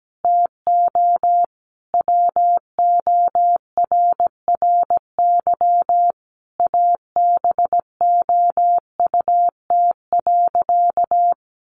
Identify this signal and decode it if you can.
Morse code, heard as a keyed tone, about 17 wpm, 700 Hz. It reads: TO WORRY ABOUT.
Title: Sentences from Top 1100 Words 17wpm